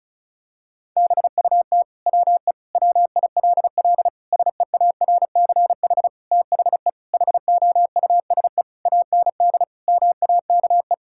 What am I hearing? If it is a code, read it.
Morse code, heard as a keyed tone, about 34 wpm, 700 Hz. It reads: BUT WE WILL SEARCH THE HOUSE AND MAKE
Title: Sentences from Top 1000 Words - Review Part 5 - 35wpm